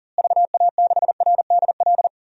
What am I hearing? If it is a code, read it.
Morse code, heard as a keyed tone, about 40 wpm, 700 Hz. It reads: VA6RDL
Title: DXCC-100 Call Signs - 6 Characters - Repeated 3x - 40wpm